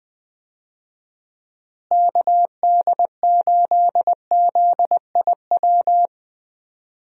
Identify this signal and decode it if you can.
Morse code, heard as a keyed tone, about 20 wpm, 700 Hz. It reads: KD8ZIW
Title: USA Call Signs - 6 Characters - Repeated 2x - 20wpm